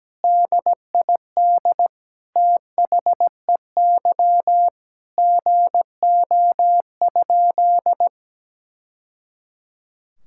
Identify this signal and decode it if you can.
Morse code, heard as a keyed tone, about 17 wpm, 700 Hz. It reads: DID THEY GO?
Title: Sentences from Top 1000 Words - Review Part 5 - 17wpm